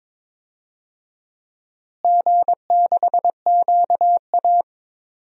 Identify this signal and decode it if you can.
Morse code, heard as a keyed tone, about 22 wpm, 700 Hz. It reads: G6QA
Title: DXCC-100 Call Signs - 4 Characters - Repeated 2x - 22wpm